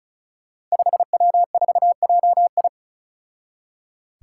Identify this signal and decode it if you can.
Morse code, heard as a keyed tone, about 36 wpm, 700 Hz. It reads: 5W4JI
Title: Five Challenging Letter-Number Combinations 35wpm